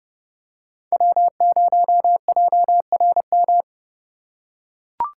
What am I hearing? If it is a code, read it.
Morse code, heard as a keyed tone, about 28 wpm, 700 Hz. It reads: W0JRM
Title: USA Call Signs - 5 Characters - 2x Character Spacing - 30wpm